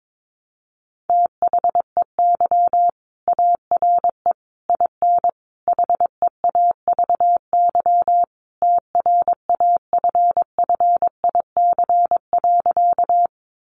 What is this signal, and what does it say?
Morse code, heard as a keyed tone, about 22 wpm, 700 Hz. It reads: THEY ARE IN HEAVY TRAFFIC.
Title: Sentences from Top 1300 Words 22wpm